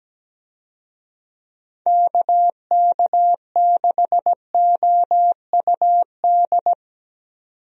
Morse code, heard as a keyed tone, about 17 wpm, 700 Hz. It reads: KK6OUD
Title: USA Call Signs - 6 Characters - 17wpm